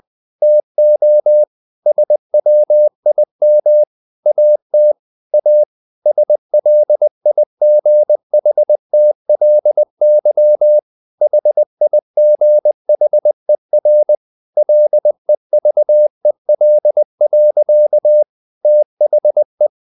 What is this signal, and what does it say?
Morse code, heard as a keyed tone, about 20 wpm, 600 Hz. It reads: TO SWIM AT A SLIGHTLY HIGHER LEVEL. THE
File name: the old man and the sea / the old man and the sea part 1 20wpm